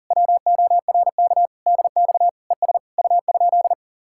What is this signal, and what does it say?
Morse code, heard as a keyed tone, about 47 wpm, 700 Hz. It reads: WORK DX ES U?
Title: Speed-Racing Common QSO Phrases - 40wpm